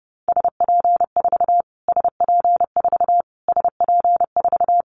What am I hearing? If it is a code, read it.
Morse code, heard as a keyed tone, about 30 wpm, 700 Hz. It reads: SP4 SP4 SP4
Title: Callsign Prefixes for Top 25 DXCC - Repeated 3x - 30wpm